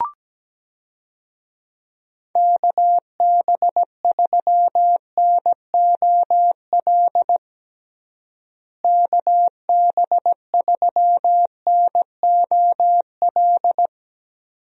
Morse code, heard as a keyed tone, about 17 wpm, 700 Hz. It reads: KB3NOL KB3NOL
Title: USA Call Signs - 6 Characters - Repeated 2x - 17wpm